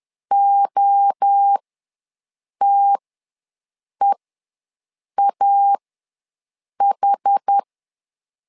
Morse code, heard as a keyed tone, about 10 wpm, spaced as for 6 wpm, 800 Hz. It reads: OTEAH
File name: LEZIONI / 3 MWRO / 3 MWRO ws 10wpm 2310